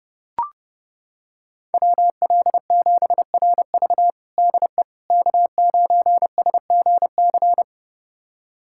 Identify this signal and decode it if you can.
Morse code, heard as a keyed tone, about 30 wpm, 700 Hz. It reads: WL7RV DE K9SGC
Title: USA Callsign DE Callsign - 5 Characters - 30wpm